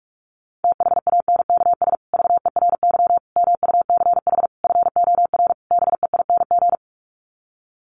Morse code, heard as a keyed tone, about 44 wpm, 700 Hz. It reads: THANKS VERY MUCH FOR BEING
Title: Sets of 5 Words Encore 45wpm